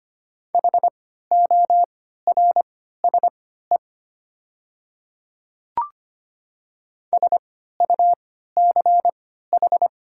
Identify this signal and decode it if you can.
Morse code, heard as a keyed tone, about 25 wpm, spaced as for 14 wpm, 700 Hz. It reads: HORSE SUCH
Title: 3x Character Spacing - Top 300 Words - 25wpm